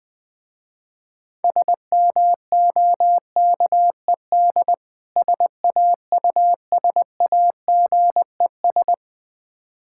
Morse code, heard as a keyed tone, about 20 wpm, 690 Hz. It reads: SMOKED SAUSAGES
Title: Difficult Words with Hints - 20wpm